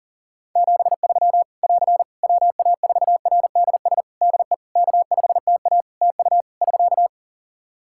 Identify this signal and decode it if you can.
Morse code, heard as a keyed tone, about 40 wpm, 700 Hz. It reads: 73 <AR> WA4RDS DE K5TA TU <SK>
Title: Practice QSO - Exchange 4 - Ending - 40wpm